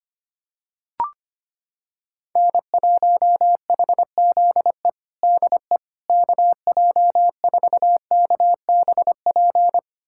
Morse code, heard as a keyed tone, about 25 wpm, 700 Hz. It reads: N1HZE DE KJ4KBP
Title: USA Callsign DE Callsign 25wpm